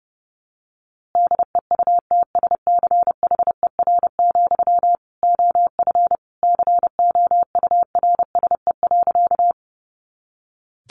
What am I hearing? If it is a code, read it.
Morse code, heard as a keyed tone, about 30 wpm, 700 Hz. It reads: DEUTSCHER, OF COURSE.
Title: A Sound Of Thunder 30wpm